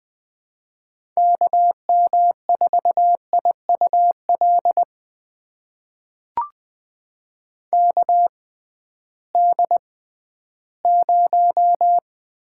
Morse code, heard as a keyed tone, about 20 wpm, spaced as for 13 wpm, 700 Hz. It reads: KM4IUL KD0
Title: USA Call Signs - 6 Characters - 6x Character Spacing - 20wpm